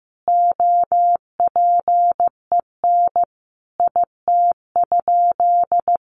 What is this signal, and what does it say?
Morse code, heard as a keyed tone, about 15 wpm, 700 Hz. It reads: OPEN IT?
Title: Sentences from Top 600 Words 15wpm